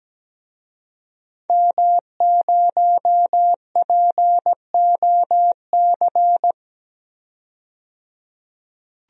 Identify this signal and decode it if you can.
Morse code, heard as a keyed tone, about 17 wpm, 700 Hz. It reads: M0POC
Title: DXCC-100 Call Signs - 5 Characters - 17wpm